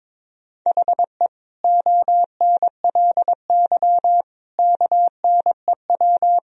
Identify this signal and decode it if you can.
Morse code, heard as a keyed tone, about 22 wpm, 700 Hz. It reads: HE ONLY KNEW
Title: Sentences from Top 1000 Words - Review Part 2 - 22wpm